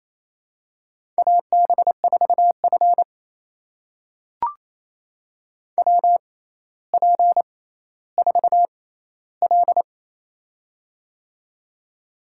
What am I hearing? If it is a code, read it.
Morse code, heard as a keyed tone, about 28 wpm, spaced as for 14 wpm, 700 Hz. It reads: AB4F WP4L
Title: USA Call Signs - 4 Characters - 6x Character Spacing - 28wpm